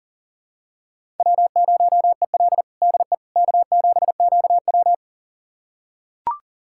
Morse code, heard as a keyed tone, about 40 wpm, 700 Hz. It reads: W0EL DE K7QW
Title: USA Callsign DE Callsign - 4 Characters - 40wpm